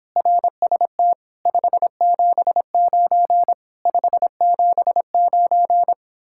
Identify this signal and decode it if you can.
Morse code, heard as a keyed tone, about 31 wpm, 700 Hz. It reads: RST 579 579